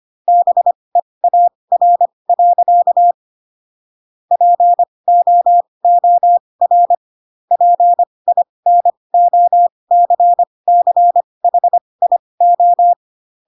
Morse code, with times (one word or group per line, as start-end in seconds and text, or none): 0.3-3.1 BEAR.
4.3-6.9 POOR
7.5-12.9 PINOCCHIO